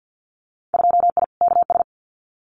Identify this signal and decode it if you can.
Morse code, heard as a keyed tone, about 50 wpm, 700 Hz. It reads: 2I KS
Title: Winter Field Day Exchanges Encore 50wpm